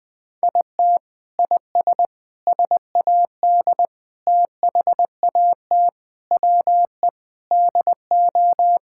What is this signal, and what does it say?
Morse code, heard as a keyed tone, about 20 wpm, 700 Hz. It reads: IT IS SAD THAT WE DO
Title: Sentences with 4-Word Phrases - 20wpm